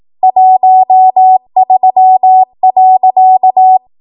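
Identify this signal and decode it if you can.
Morse code, heard as a keyed tone, about 18 wpm, 750 Hz. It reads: 13.
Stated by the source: ARRL 18 wpm Morse code practice archive transmission.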